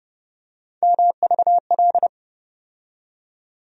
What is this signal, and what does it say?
Morse code, heard as a keyed tone, about 30 wpm, 700 Hz. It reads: MVL
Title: USA Call Sign Postfixes - 3 Letters 30wpm